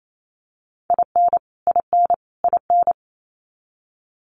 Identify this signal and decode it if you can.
Morse code, heard as a keyed tone, about 28 wpm, 700 Hz. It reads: IN IN IN
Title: Top 500 Words Repeated 28wpm